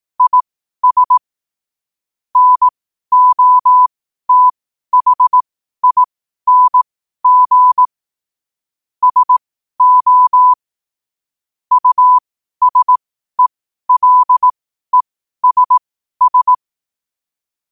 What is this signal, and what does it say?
Morse code, heard as a keyed tone, about 18 wpm, spaced as for 12 wpm, 1000 Hz. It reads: IS NOTHING SO USELESS